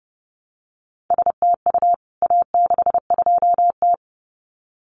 Morse code, heard as a keyed tone, about 30 wpm, 700 Hz. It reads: STU A62T
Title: CWT Exchange 30wpm